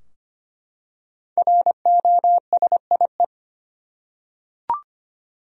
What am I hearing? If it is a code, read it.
Morse code, heard as a keyed tone, about 25 wpm, 700 Hz. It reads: ROSIE
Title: USA Common Names - Max 5 Letters - 25wpm